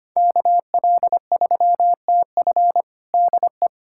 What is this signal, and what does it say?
Morse code, heard as a keyed tone, about 25 wpm, 700 Hz. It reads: KL3TF DE
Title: USA Callsign DE Callsign - 5 Characters - 25wpm